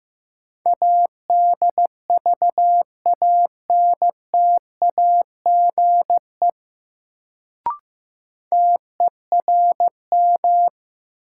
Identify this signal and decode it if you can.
Morse code, heard as a keyed tone, about 16 wpm, 700 Hz. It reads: ADVANTAGE TERM
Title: Sentences from Top 900 Words - New Word Drill - 15wpm